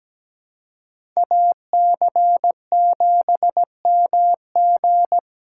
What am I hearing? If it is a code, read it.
Morse code, heard as a keyed tone, about 17 wpm, 700 Hz. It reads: AC7MG
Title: USA Call Signs - 5 Characters - 4x Character Spacing - 17wpm